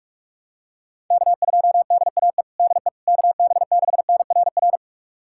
Morse code, heard as a keyed tone, about 42 wpm, 690 Hz. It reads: K2DAE DE KB6NRR
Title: USA Callsign DE Callsign 45wpm